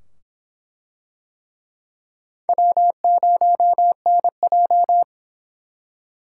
Morse code, heard as a keyed tone, about 24 wpm, 700 Hz. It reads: W0NJ